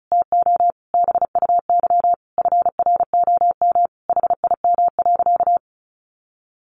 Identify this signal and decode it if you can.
Morse code, heard as a keyed tone, about 34 wpm, 700 Hz. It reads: TO BUY FROM HIM.
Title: Sentences from Top 1200 Words 35wpm